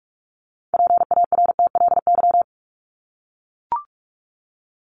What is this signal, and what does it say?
Morse code, heard as a keyed tone, about 44 wpm, 700 Hz. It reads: PARTLY
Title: LY Suffix - Top 100 Words with Base Words 45wpm